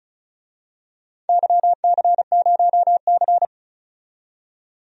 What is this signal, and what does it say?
Morse code, heard as a keyed tone, about 36 wpm, 700 Hz. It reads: YC0C
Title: DXCC-100 Call Signs - 4 Characters - 35wpm